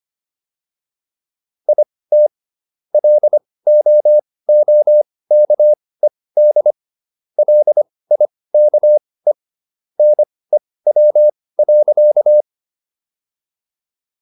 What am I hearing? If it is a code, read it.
Morse code, heard as a keyed tone, about 25 wpm, spaced as for 18 wpm, 600 Hz. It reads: IT LOOKED LIKE NEW.